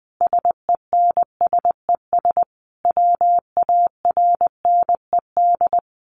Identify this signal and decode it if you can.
Morse code, heard as a keyed tone, about 20 wpm, 700 Hz. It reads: SENSES WARNED
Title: A Sound Of Thunder 20wpm